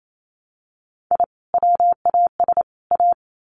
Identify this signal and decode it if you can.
Morse code, heard as a keyed tone, about 28 wpm, 700 Hz. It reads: I WAS A
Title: Sets of 3 Words 28wpm